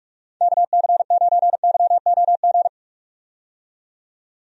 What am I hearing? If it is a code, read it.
Morse code, heard as a keyed tone, about 45 wpm, 700 Hz. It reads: KC9YOG
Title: USA Call Signs - 6 Characters - 45wpm